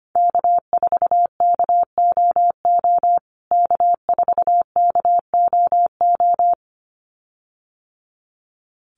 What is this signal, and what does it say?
Morse code, heard as a keyed tone, about 25 wpm, 700 Hz. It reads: K4KOO K4KOO
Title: USA Call Signs Repeated 2x 25wpm